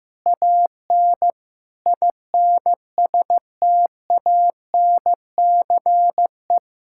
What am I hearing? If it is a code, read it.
Morse code, heard as a keyed tone, about 15 wpm, 700 Hz. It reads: AN INSTANCE
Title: Sentences from Top 1000 Words - Review Part 5 - 15wpm